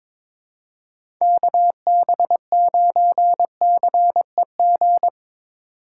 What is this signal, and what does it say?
Morse code, heard as a keyed tone, about 22 wpm, 700 Hz. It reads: KB9CEG
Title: USA Call Signs - 6 Characters - 3x Character Spacing - 22wpm